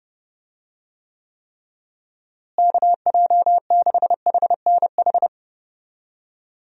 Morse code, heard as a keyed tone, about 30 wpm, 700 Hz. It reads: KJ6HNH
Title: USA Call Signs - 6 Characters - Repeated 3x - 30wpm